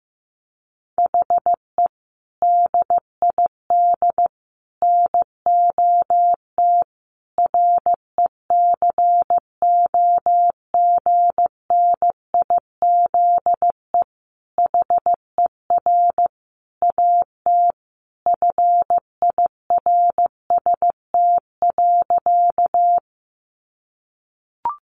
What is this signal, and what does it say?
Morse code, heard as a keyed tone, about 15 wpm, 700 Hz. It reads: HE DID NOT RECOGNIZE HER AT FIRST.
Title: Sentences from Top 1400 Words 15wpm